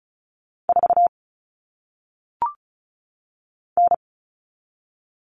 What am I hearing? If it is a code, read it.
Morse code, heard as a keyed tone, about 36 wpm, 700 Hz. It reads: 4 N
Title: All Cut Numbers - 1-Digit 35wpm